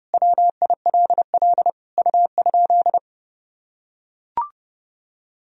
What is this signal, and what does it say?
Morse code, heard as a keyed tone, about 32 wpm, 700 Hz. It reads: WILL U?
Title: Sentences from Top 400 Words and Abbreviations 30wpm